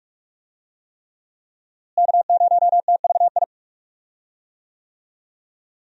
Morse code, heard as a keyed tone, about 46 wpm, 700 Hz. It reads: K0TVI
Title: USA Call Signs - 5 Characters - Repeated 3x - 45wpm